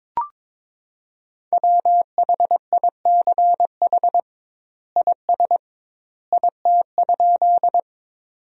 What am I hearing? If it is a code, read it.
Morse code, heard as a keyed tone, about 22 wpm, 700 Hz. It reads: WHICH IS IT?
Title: 2x Word-Spacing - Sentences from Top 100 Words - 22wpm